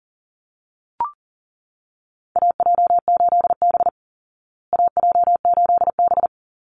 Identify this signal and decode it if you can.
Morse code, heard as a keyed tone, about 40 wpm, 700 Hz. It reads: AJ8B AJ8B
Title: Call Signs Repeated 2x - 40wpm